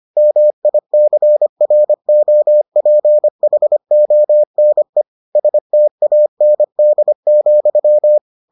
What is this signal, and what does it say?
Morse code, heard as a keyed tone, about 25 wpm, 600 Hz. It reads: MICROPHONE STAND,